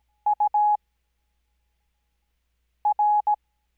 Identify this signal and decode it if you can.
Morse code, heard as a keyed tone, about 17 wpm, spaced as for 4 wpm, 840 Hz. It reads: UR
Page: Farnsworth Spacing: 15WPM Character speed Spaced at 5WPM.